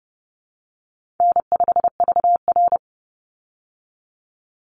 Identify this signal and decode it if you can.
Morse code, heard as a keyed tone, about 30 wpm, 700 Hz. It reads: N5VR
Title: Easy Call Signs Repeated 3x - 30wpm